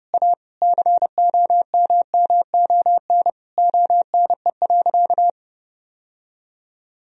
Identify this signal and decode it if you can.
Morse code, heard as a keyed tone, about 32 wpm, 700 Hz. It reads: A COMMON ONE.
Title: Sentences from Top 1000 Words - Review Part 4 - 30wpm